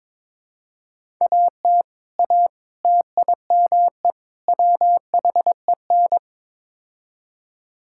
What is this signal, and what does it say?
Morse code, heard as a keyed tone, about 22 wpm, 700 Hz. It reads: AT A TIME WHEN
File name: practice / 22wpm / Build A Phrase - 5-Word Phrases - 22wpm